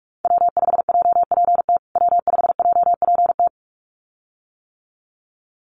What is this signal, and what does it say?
Morse code, heard as a keyed tone, about 44 wpm, 700 Hz. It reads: W5JPT W5JPT
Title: USA Call Signs Repeated 2x 45wpm